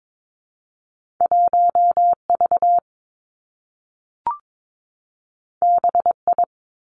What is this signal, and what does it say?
Morse code, heard as a keyed tone, about 22 wpm, 700 Hz. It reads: 1 V B I
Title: Two Challenging Letter-Number Combinations 22wpm